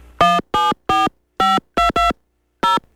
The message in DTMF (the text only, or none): 174522 0